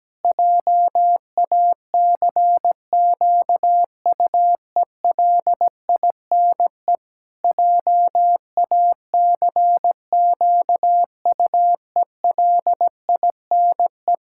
Morse code, heard as a keyed tone, about 17 wpm, 700 Hz. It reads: JACQUELINE JACQUELINE
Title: Practice QSO - Exchange 1 - Beginning - 17wpm